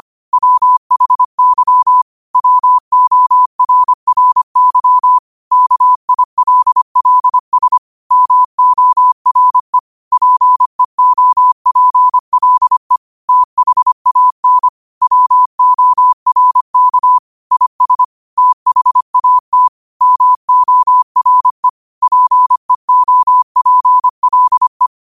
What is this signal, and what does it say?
Morse code, heard as a keyed tone, about 25 wpm, 1000 Hz. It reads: WHY WORRY KILLS MORE PEOPLE THAN WORK IS THAT MORE PEOPLE
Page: New quotes every day in morse code at 25 Words per minute.